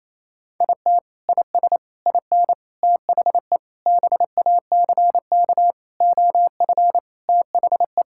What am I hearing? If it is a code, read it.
Morse code, heard as a keyed tone, about 28 wpm, 700 Hz. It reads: IT IS IN THE BACK OF THE
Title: Sentences from Top 1000 Words - Review Part 5 - 28wpm